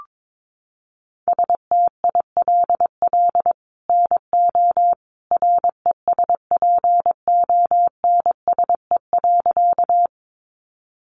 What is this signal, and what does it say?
Morse code, heard as a keyed tone, about 22 wpm, 700 Hz. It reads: STILL NO RESPONSE.
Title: Sentences from Top 1400 Words 22wpm